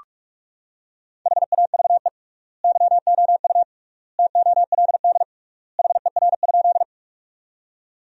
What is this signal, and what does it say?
Morse code, heard as a keyed tone, about 47 wpm, 700 Hz. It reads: HAVE YOU TOLD HER?
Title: 3x Word-Spacing - Sentences from Top 400 Words - 45wpm